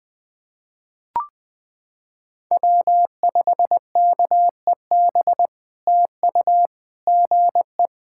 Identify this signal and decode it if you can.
Morse code, heard as a keyed tone, about 20 wpm, 700 Hz. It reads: W5KEB TU GE
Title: POTA QSO Practice - 20wpm